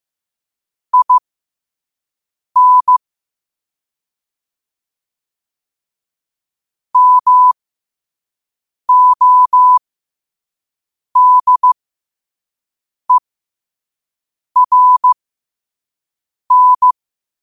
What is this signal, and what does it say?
Morse code, heard as a keyed tone, about 15 wpm, spaced as for 5 wpm, 1000 Hz. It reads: IN MODERN
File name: qotdfiles / 5wpm